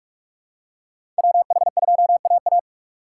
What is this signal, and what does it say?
Morse code, heard as a keyed tone, about 46 wpm, 700 Hz. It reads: WH2AA
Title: USA Call Signs - 5 Characters - 3x Character Spacing - 45wpm